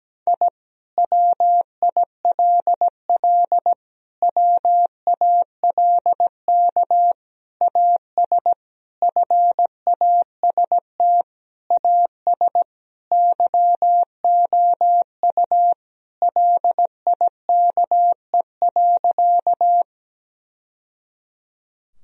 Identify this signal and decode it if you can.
Morse code, heard as a keyed tone, about 17 wpm, 700 Hz. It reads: I WILL WALK AS FAST AS YOU LIKE.
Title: Sentences from Top 1000 Words - Review Part 2 - 17wpm